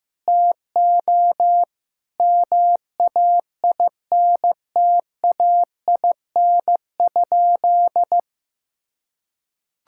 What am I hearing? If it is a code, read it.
Morse code, heard as a keyed tone, about 15 wpm, 700 Hz. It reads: TO MAINTAIN?
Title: Sentences from Top 1000 Words - Review Part 5 - 15wpm